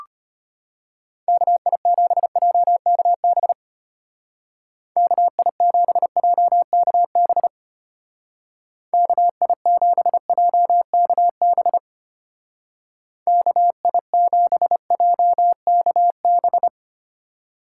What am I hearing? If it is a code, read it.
Morse code, heard as a keyed tone, about 32 wpm, 700 Hz. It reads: KI7JKB KI7JKB KI7JKB KI7JKB